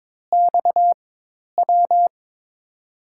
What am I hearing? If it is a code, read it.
Morse code, heard as a keyed tone, about 21 wpm, spaced as for 10 wpm, 700 Hz. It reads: XW
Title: USA Call Signs - 6 Characters - 4x Character Spacing - 22wpm